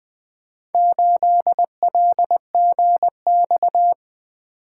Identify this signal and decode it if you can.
Morse code, heard as a keyed tone, about 20 wpm, 700 Hz. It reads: 8LGX
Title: Four Letter-Number Combinations - 20wpm